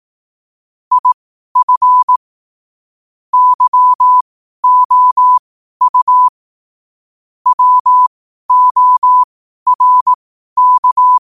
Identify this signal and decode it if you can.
Morse code, heard as a keyed tone, about 18 wpm, spaced as for 12 wpm, 1000 Hz. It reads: IF YOU WORK
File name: qotdfiles / 12wpm